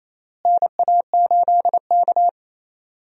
Morse code, heard as a keyed tone, about 28 wpm, 700 Hz. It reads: NA8K
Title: USA Call Signs - 4 Characters - Repeated 3x - 28wpm